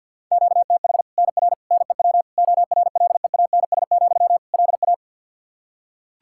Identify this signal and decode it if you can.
Morse code, heard as a keyed tone, about 50 wpm, 700 Hz. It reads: QTH NR NEW ORLEANS, LA
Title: K1USN SST QSO Practice 50wpm